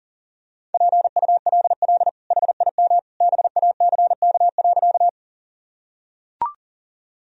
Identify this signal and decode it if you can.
Morse code, heard as a keyed tone, about 37 wpm, 700 Hz. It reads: PULL HIM BACK.
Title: Sentences from Top 1000 Words - Review Part 2 - 40wpm